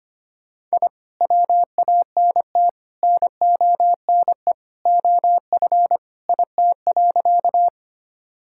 Morse code, heard as a keyed tone, about 25 wpm, 700 Hz. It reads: I WANT NONE OF IT.
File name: practice / 25wpm / Sentences with 2-Word Phrases - 25wpm